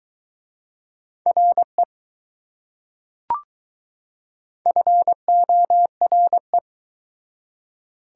Morse code, heard as a keyed tone, about 21 wpm, 690 Hz. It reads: RE FORE